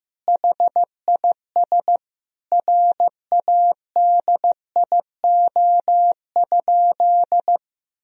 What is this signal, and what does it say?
Morse code, heard as a keyed tone, about 15 wpm, 700 Hz. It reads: HIS RADIO?
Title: Sentences from Top 1000 Words - Review Part 3 - 15wpm